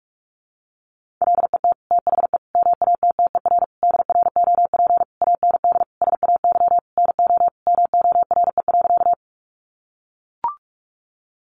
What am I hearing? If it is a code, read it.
Morse code, heard as a keyed tone, about 44 wpm, 700 Hz. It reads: LET THE MATTER DROP AND SAY NO MORE.
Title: Sentences from Top 1400 Words 45wpm